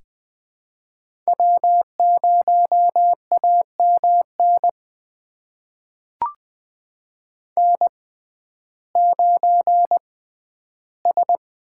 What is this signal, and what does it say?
Morse code, heard as a keyed tone, about 20 wpm, spaced as for 12 wpm, 700 Hz. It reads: W0AMN N9S
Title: USA Call Signs - 5 Characters - 6x Character Spacing - 20wpm